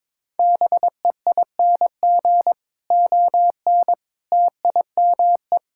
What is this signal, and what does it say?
Morse code, heard as a keyed tone, about 23 wpm, 700 Hz. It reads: BEING ON TIME
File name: Difficult Sentences with Hints - 22wpm